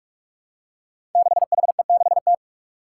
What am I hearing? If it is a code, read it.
Morse code, heard as a keyed tone, about 46 wpm, 700 Hz. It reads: 6HE6T
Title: Five Challenging Letter-Number Combinations 45wpm